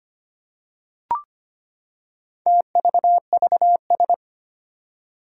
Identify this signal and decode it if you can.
Morse code, heard as a keyed tone, about 25 wpm, 700 Hz. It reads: TVVS
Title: Four Challenging Letter-Number Combinations 25wpm